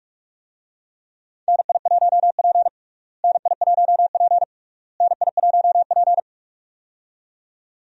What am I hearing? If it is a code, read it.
Morse code, heard as a keyed tone, about 42 wpm, 700 Hz. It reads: NI1P NI1P NI1P
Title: USA Call Signs - 4 Characters - Repeated 3x - 45wpm